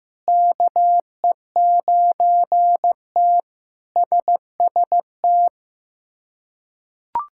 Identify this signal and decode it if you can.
Morse code, heard as a keyed tone, about 16 wpm, 700 Hz. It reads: KE9T SST
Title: K1USN SST QSO Practice - 15wpm